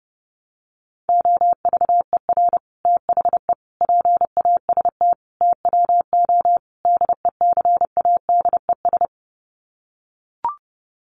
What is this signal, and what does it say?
Morse code, heard as a keyed tone, about 30 wpm, 700 Hz. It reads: OVER THE PAST TWO DECADES
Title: Sets of 5 Words Encore2 30wpm